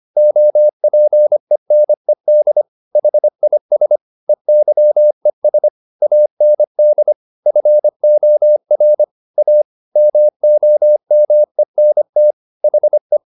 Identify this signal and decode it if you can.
Morse code, heard as a keyed tone, about 25 wpm, 600 Hz. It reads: OPENED HIS EYES AND FOR A MOMENT HE
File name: the old man and the sea part 1 25wpm